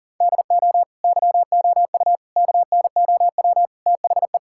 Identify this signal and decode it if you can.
Morse code, heard as a keyed tone, about 40 wpm, 700 Hz. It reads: DO YOU KNOW THE
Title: Sentences from Top 1000 Words - Review Part 6 - 40wpm